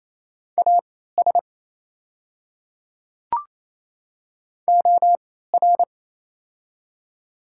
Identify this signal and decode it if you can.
Morse code, heard as a keyed tone, about 28 wpm, spaced as for 16 wpm, 700 Hz. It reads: AS OR
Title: 3x Character Spacing - Top 100 Words - 28wpm